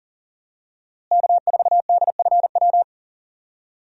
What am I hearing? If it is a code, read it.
Morse code, heard as a keyed tone, about 40 wpm, 700 Hz. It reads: K4DFW
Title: USA Call Signs - 5 Characters - Repeated 2x - 40wpm